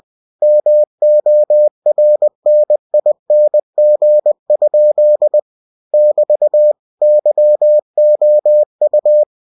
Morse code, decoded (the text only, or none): MORNING? <BT> YOU